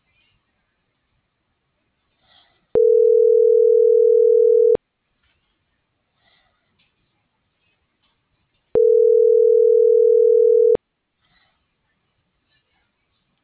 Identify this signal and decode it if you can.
Ringback tone (North America), 440 + 480 Hz.